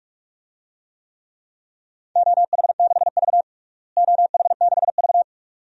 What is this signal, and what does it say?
Morse code, heard as a keyed tone, about 46 wpm, 700 Hz. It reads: OH6V OH6V
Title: DXCC-100 Call Signs - 4 Characters - Repeated 2x - 45wpm